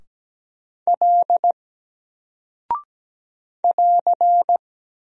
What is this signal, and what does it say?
Morse code, heard as a keyed tone, about 17 wpm, 700 Hz. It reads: L <AR>
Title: Single Character (Letter:Number:Punctuation:Prosign) 17wpm